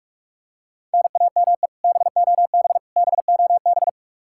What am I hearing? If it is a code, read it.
Morse code, heard as a keyed tone, about 41 wpm, 700 Hz. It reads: NAME BOB BOB
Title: K1USN SST QSO Practice 45wpm